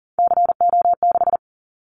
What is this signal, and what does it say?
Morse code, heard as a keyed tone, about 40 wpm, 700 Hz. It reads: CO6
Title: Callsign Prefixes for Top 100 DXCC - 40wpm